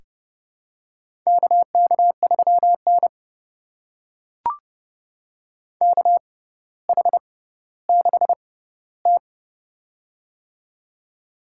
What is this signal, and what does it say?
Morse code, heard as a keyed tone, about 32 wpm, spaced as for 16 wpm, 700 Hz. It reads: KK3N KH6T
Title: USA Call Signs - 4 Characters - 6x Character Spacing - 30wpm